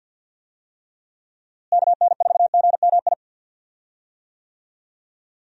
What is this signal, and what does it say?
Morse code, heard as a keyed tone, about 50 wpm, 700 Hz. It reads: KN4GMI
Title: USA Call Signs - 6 Characters - Repeated 3x - 50wpm